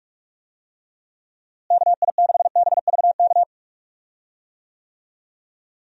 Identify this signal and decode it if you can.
Morse code, heard as a keyed tone, about 47 wpm, 700 Hz. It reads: KI6BVK
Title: USA Call Signs - 6 Characters - Repeated 3x - 45wpm